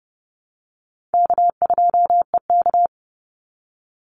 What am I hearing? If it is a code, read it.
Morse code, heard as a keyed tone, about 29 wpm, 700 Hz. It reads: K2EK
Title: Easy Call Signs - 30wpm